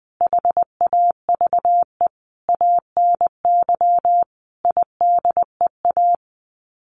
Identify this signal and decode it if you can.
Morse code, heard as a keyed tone, about 20 wpm, 700 Hz. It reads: HAVE ANY IDEA
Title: Sets of 5 Words Encore 20wpm